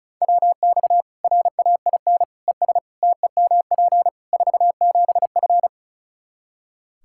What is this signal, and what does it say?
Morse code, heard as a keyed tone, about 36 wpm, 700 Hz. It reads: WX RAIN ES TEMP 47F
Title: Practice QSO - Exchange 2 - Info - 35wpm